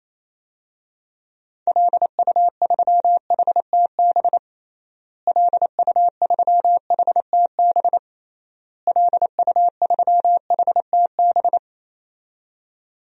Morse code, heard as a keyed tone, about 28 wpm, 700 Hz. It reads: LU3HTB LU3HTB LU3HTB
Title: DXCC-100 Call Signs - 6 Characters - Repeated 3x - 28wpm